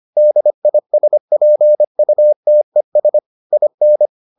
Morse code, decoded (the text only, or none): DISPUTES IN